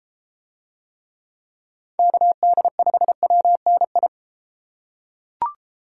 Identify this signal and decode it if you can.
Morse code, heard as a keyed tone, about 30 wpm, 700 Hz. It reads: KD5WNI